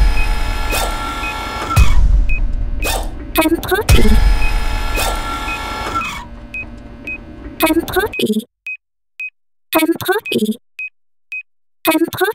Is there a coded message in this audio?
No, only speech.